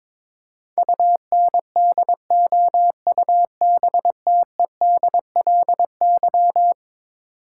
Morse code, heard as a keyed tone, about 22 wpm, 700 Hz. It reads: UNDOUBTEDLY
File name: practice / 22wpm / Eleven Letter Words - 22wpm